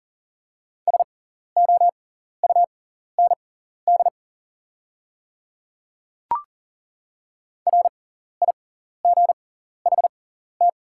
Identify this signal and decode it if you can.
Morse code, heard as a keyed tone, about 40 wpm, spaced as for 14 wpm, 700 Hz. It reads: SOUND RIGHT
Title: 6x Character Spacing - Top 200 Words - 40wpm